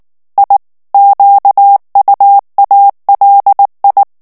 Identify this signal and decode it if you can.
Morse code, heard as a keyed tone, about 19 wpm, 800 Hz. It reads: I QUALI
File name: morse mp3 / ITALIEN texte clair 2 vitesse 18 mots minute